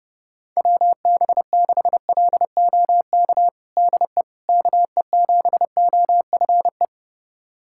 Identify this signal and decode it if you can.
Morse code, heard as a keyed tone, about 30 wpm, 700 Hz. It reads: WB6LOK DE KE7OFE
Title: USA Callsign DE Callsign - 6 Characters - 30wpm